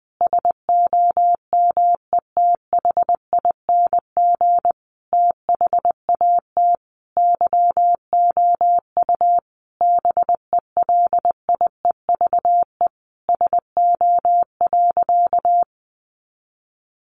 Morse code, heard as a keyed tone, about 20 wpm, 700 Hz. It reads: SOMETHING THAT YOU BELIEVE SO.
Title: Sentences from Top 1000 Words 20wpm